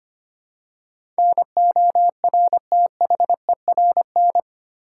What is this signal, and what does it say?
Morse code, heard as a keyed tone, about 25 wpm, 700 Hz. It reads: NORTHERN
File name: Eight Letter Words - 25wpm